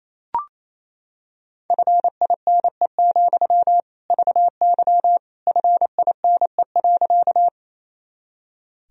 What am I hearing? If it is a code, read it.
Morse code, heard as a keyed tone, about 28 wpm, 700 Hz. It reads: FINE, VY FINE.
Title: Sentences from Top 500 Words and Abbreviations 28wpm